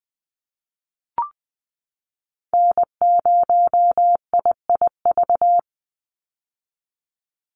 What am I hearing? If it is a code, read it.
Morse code, heard as a keyed tone, about 19 wpm, 700 Hz. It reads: N0IIV
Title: USA Call Signs 20wpm